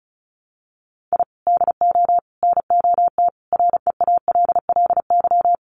Morse code, heard as a keyed tone, about 34 wpm, 700 Hz. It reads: I DO NOT REALLY
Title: Sets of 5 Words Encore 35wpm